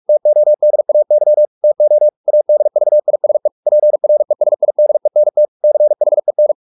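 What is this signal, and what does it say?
Morse code, heard as a keyed tone, about 44 wpm, 600 Hz. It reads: TODAY TO ADVISE PRESIDENT CHEN
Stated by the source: Morse practice at 45 wpm, the novel Never by Ken Follett in English, from CW med Gnister.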